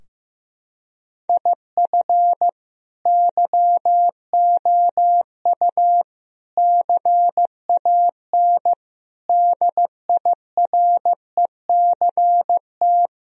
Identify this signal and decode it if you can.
Morse code, heard as a keyed tone, about 15 wpm, 700 Hz. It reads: IF YOU CAN DIRECT
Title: Sentences from Top 1000 Words - Review Part 2 - 15wpm